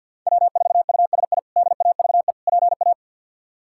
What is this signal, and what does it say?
Morse code, heard as a keyed tone, about 50 wpm, 700 Hz. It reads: W4USI DAVE PA
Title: K1USN SST QSO Practice - 50wpm